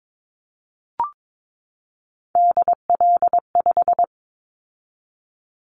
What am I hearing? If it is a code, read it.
Morse code, heard as a keyed tone, about 22 wpm, 700 Hz. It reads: DL5
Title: Callsign Prefixes for Top 50 DXCC - 22wpm